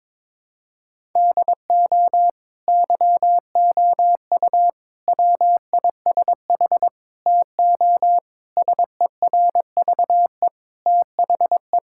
Morse code, heard as a keyed tone, about 22 wpm, 700 Hz. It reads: DO YOU WISH TO SERVE THE
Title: Sentences from Top 1600 Words 22wpm